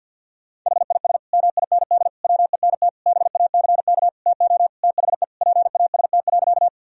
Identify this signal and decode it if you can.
Morse code, heard as a keyed tone, about 51 wpm, 700 Hz. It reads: HIS MIND WENT BACK TO THE PAST.
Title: Sentences from Top 1000 Words - Review Part 3 - 50wpm